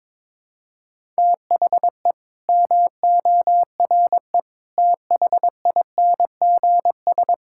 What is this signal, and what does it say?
Morse code, heard as a keyed tone, about 22 wpm, 700 Hz. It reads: THE MORE THINGS
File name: practice / 22wpm / Sets of 8 Words - 22wpm